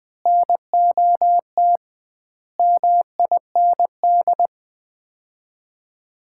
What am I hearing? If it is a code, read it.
Morse code, heard as a keyed tone, about 20 wpm, 700 Hz. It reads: NOT MIND
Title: 2x Word-Spacing - Sets of 5 Words - 20wpm